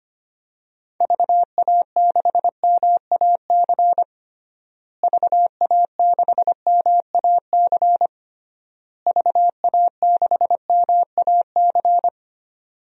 Morse code, heard as a keyed tone, about 25 wpm, 700 Hz. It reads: VA6MAC VA6MAC VA6MAC
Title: DXCC-100 Call Signs - 6 Characters - Repeated 3x - 25wpm